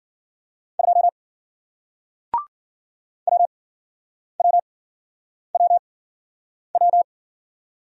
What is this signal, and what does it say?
Morse code, heard as a keyed tone, about 53 wpm, spaced as for 9 wpm, 700 Hz. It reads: 2 WWWW